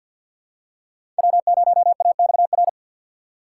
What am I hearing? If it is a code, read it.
Morse code, heard as a keyed tone, about 51 wpm, 700 Hz. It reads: W 0 A X R
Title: USA Call Signs - 6x Character Spacing - 50wpm